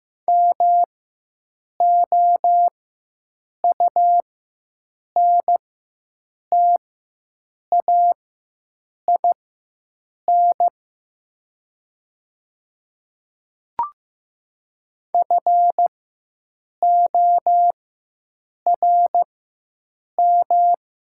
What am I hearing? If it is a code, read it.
Morse code, heard as a keyed tone, about 15 wpm, spaced as for 7 wpm, 700 Hz. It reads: MOUNTAIN FORM
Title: 4x Character Spacing - Top 300 Words - 15wpm